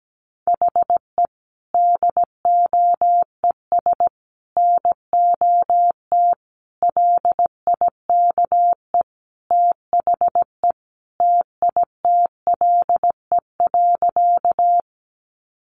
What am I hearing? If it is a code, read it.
Morse code, heard as a keyed tone, about 17 wpm, 700 Hz. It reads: HE DOES NOT LIKE THE TITLE.
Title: Sentences from Top 1200 Words 17wpm